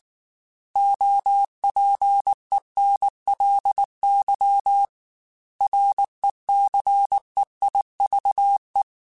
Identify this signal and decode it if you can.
Morse code, heard as a keyed tone, about 19 wpm, 800 Hz. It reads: OPENLY RECEIVE
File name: Quotes / Quotes-mp3s / Advice @18